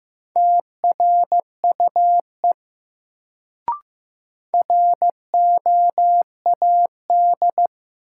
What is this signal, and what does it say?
Morse code, heard as a keyed tone, about 15 wpm, 700 Hz. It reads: TRUE ROAD
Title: Sentences from Top 400 Words - New Word Drill - 15wpm